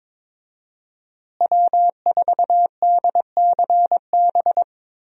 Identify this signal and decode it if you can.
Morse code, heard as a keyed tone, about 22 wpm, 700 Hz. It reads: W4DCB
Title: USA Call Signs - 5 Characters - Repeated 3x - 22wpm